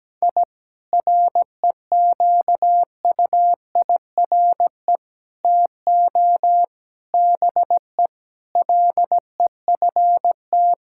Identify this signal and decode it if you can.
Morse code, heard as a keyed tone, about 17 wpm, 700 Hz. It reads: I REQUIRE TO BE LEFT
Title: Sentences from Top 1000 Words - Review Part 6 - 17wpm